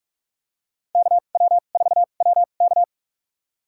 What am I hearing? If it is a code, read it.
Morse code, heard as a keyed tone, about 46 wpm, spaced as for 33 wpm, 700 Hz. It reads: KW4WK
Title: USA Call Signs - 5 Characters - 2x Character Spacing - 45wpm